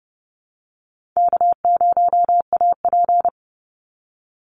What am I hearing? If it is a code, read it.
Morse code, heard as a keyed tone, about 30 wpm, 700 Hz. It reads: K0AP
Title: Call Signs Repeated 2x - 30wpm